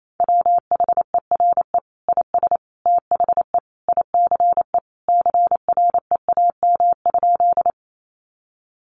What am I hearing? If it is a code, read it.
Morse code, heard as a keyed tone, about 28 wpm, 700 Hz. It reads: WHERE IS THE ICE CREAM?
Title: Sentences from Top 1500 Words 28wpm